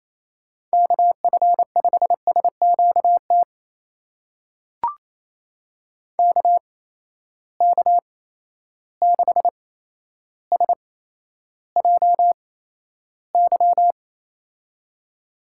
Morse code, heard as a keyed tone, about 28 wpm, spaced as for 12 wpm, 700 Hz. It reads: KF5SQT KK6SJY
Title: USA Call Signs - 6 Characters - 8x Character Spacing - 28wpm